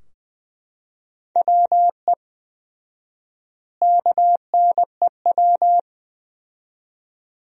Morse code, heard as a keyed tone, about 20 wpm, 700 Hz. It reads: WE KNEW
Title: 4x Word-Spacing - Sentences from Top 400 Words - 20wpm